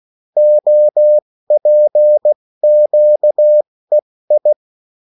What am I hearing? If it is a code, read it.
Morse code, heard as a keyed tone, about 16 wpm, spaced as for 14 wpm, 600 Hz. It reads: OPQEI
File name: morse mp3 / revision9-vitesse 14 mots